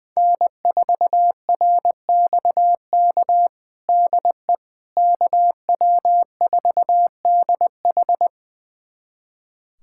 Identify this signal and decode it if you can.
Morse code, heard as a keyed tone, about 20 wpm, 700 Hz. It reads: N4RXK DE KW4DH
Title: USA Callsign DE Callsign - 5 Characters - 20wpm